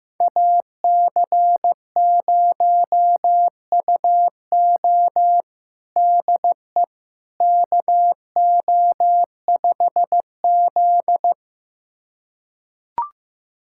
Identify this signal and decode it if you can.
Morse code, heard as a keyed tone, about 16 wpm, 700 Hz. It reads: AC0UO DE KO5Z
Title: USA Callsign DE Callsign 15wpm